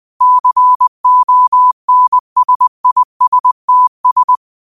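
Morse code, heard as a keyed tone, about 20 wpm, 1000 Hz. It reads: CONSISTS
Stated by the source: Quote of the day sent in Morse at 20 words per minute.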